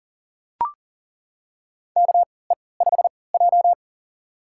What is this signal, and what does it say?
Morse code, heard as a keyed tone, about 37 wpm, spaced as for 22 wpm, 700 Hz. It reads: KE5J
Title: USA Call Signs - 4 Characters - 3x Character Spacing - 40wpm